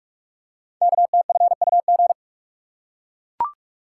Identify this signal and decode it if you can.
Morse code, heard as a keyed tone, about 44 wpm, 690 Hz. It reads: KTFUG
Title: Code Groups - 5 Letters - 45wpm